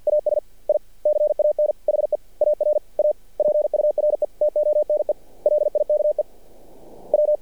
Morse code, heard as a keyed tone, about 50 wpm, 600 Hz. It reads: AS I CAN HEAR A PURE TONE LIKE W
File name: archived / CRSnet